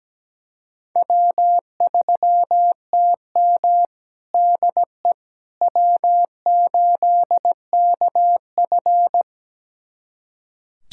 Morse code, heard as a keyed tone, about 17 wpm, 700 Hz. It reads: W3TM DE W8KF
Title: USA Callsign DE Callsign - 4 Characters - 17wpm